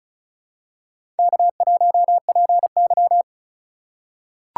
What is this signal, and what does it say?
Morse code, heard as a keyed tone, about 36 wpm, 700 Hz. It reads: K1PY
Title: USA Call Signs - 2x Character Spacing - 35wpm